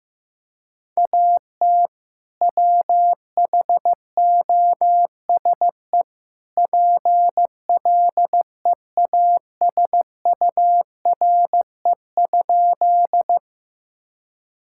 Morse code, heard as a keyed tone, about 16 wpm, 700 Hz. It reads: AT WHOSE PLEASURE?
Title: Sentences from Top 1600 Words 15wpm